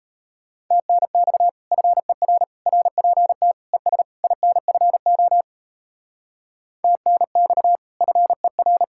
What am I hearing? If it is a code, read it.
Morse code, heard as a keyed tone, about 35 wpm, 700 Hz. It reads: TNX FER RPT ES INFO TNX FER
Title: Speed-Racing Common QSO Phrases - 28wpm